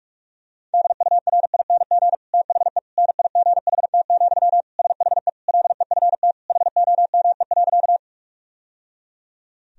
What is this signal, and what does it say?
Morse code, heard as a keyed tone, about 46 wpm, 700 Hz. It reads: DURING THE NIGHT, SHE LEFT HOME.
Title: Sentences from Top 1000 Words - Review Part 2 - 45wpm